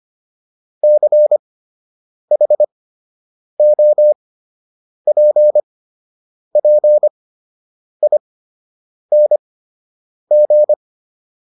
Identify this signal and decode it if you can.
Morse code, heard as a keyed tone, about 25 wpm, spaced as for 8 wpm, 600 Hz. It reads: CHOPPING
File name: ING8 08